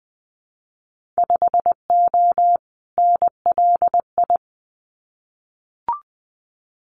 Morse code, heard as a keyed tone, about 20 wpm, 700 Hz. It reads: 5O NLI
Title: Winter Field Day Exchanges 20wpm